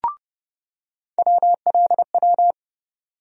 Morse code, heard as a keyed tone, about 30 wpm, 700 Hz. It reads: WLW
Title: USA Call Sign Postfixes - 3 Letters 30wpm